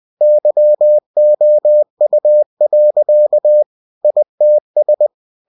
Morse code, heard as a keyed tone, about 20 wpm, 600 Hz. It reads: YOU. ITS
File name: cw-books / winnie-the-pooh / winnie-20wpm-page-0077